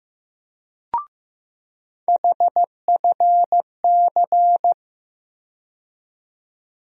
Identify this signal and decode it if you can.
Morse code, heard as a keyed tone, about 15 wpm, 700 Hz. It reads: HFC
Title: USA Call Sign Postfixes - 3 Letters 15wpm